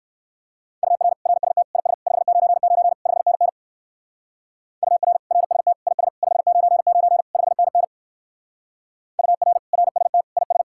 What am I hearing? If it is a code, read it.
Morse code, heard as a keyed tone, about 54 wpm, 700 Hz. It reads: UR RST IS 599 5NN UR RST IS 599 5NN UR RST IS
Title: Speed-Racing Common QSO Phrases Encore - 45wpm